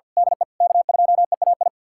Morse code, heard as a keyed tone, about 51 wpm, 700 Hz. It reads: DE K2EAI
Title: USA Callsign DE Callsign - 5 Characters - 50wpm